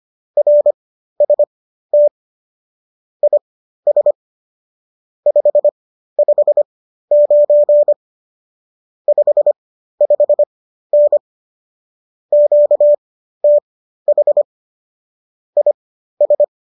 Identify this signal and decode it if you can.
Morse code, heard as a keyed tone, about 25 wpm, spaced as for 13 wpm, 590 Hz. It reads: RST IS 559 55N QTH IS